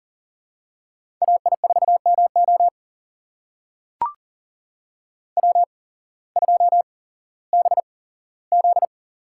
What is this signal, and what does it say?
Morse code, heard as a keyed tone, about 40 wpm, spaced as for 19 wpm, 700 Hz. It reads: AI4MO W2BZ
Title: USA Call Signs - 8x Character Spacing - 40wpm